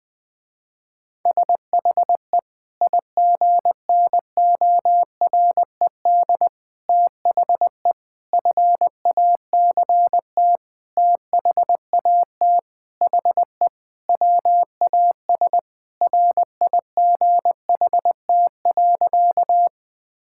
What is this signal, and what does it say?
Morse code, heard as a keyed tone, about 21 wpm, 700 Hz. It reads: SHE IGNORED THE FACT THAT HE WAS RIGHT.
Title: Sentences with 5-Word Phrases - Spoken Beginning - 20wpm